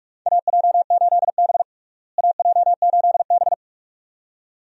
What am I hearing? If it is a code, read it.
Morse code, heard as a keyed tone, about 46 wpm, 700 Hz. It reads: AJ8B AJ8B
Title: USA Call Signs - 4 Characters - Repeated 2x - 45wpm